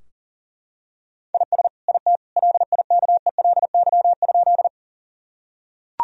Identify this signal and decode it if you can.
Morse code, heard as a keyed tone, about 37 wpm, 700 Hz. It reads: IS IT LIKELY?
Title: Sentences from Top 1000 Words - Review Part 4 - 40wpm